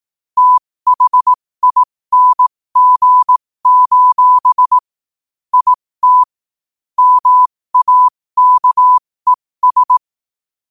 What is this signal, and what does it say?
Morse code, heard as a keyed tone, about 18 wpm, spaced as for 15 wpm, 1000 Hz. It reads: THING: IT MAKES
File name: qotdfiles / 15wpm